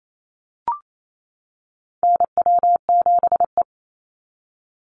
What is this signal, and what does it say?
Morse code, heard as a keyed tone, about 28 wpm, 700 Hz. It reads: NW7E
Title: NA SOTA Chaser Callsigns 28wpm